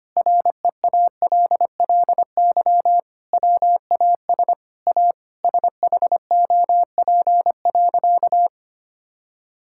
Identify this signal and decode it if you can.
Morse code, heard as a keyed tone, about 25 wpm, 700 Hz. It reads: REALLY WAS A SHOP.
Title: Sentences from Top 1600 Words 25wpm